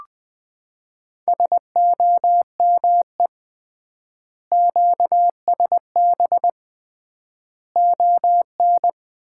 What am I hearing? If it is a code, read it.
Morse code, heard as a keyed tone, about 21 wpm, 700 Hz. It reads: SOME QSB ON
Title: Common QSO Phrases - 3x Word-Spacing - 20wpm